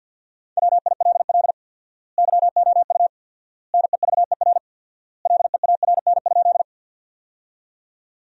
Morse code, heard as a keyed tone, about 50 wpm, 700 Hz. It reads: WILL YOU NEVER LEARN?
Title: 4x Word-Spacing - Sentences from Top 300 Words - 50wpm